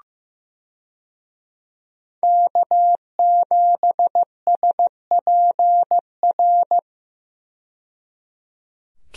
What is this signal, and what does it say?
Morse code, heard as a keyed tone, about 15 wpm, 700 Hz. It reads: K7SPR
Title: USA Call Signs - 5 Characters - 15wpm